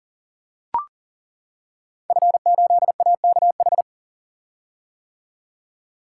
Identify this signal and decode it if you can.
Morse code, heard as a keyed tone, about 40 wpm, 700 Hz. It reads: F8AKH
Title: Code Groups - 5 Letter:Numbers - 40wpm